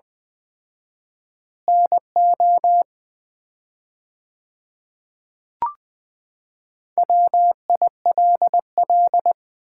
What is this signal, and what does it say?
Morse code, heard as a keyed tone, about 20 wpm, 700 Hz. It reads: NO WILL
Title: 4x Word-Spacing - Sets of 3 Words - 20wpm